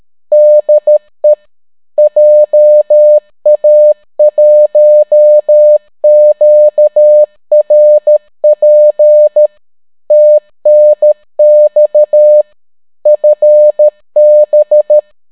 Morse code, heard as a keyed tone, about 13 wpm, 600 Hz. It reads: DE JA1QRP TNX FB